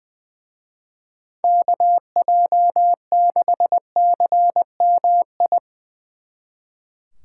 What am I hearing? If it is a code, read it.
Morse code, heard as a keyed tone, about 23 wpm, 700 Hz. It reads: KJ6CMI
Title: Speed Racing - USA Call Signs 20wpm